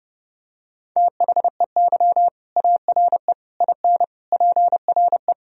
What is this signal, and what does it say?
Morse code, heard as a keyed tone, about 32 wpm, 700 Hz. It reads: THEY ARE IN PRE
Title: Sentences from Top 1000 Words - Review Part 3 - 30wpm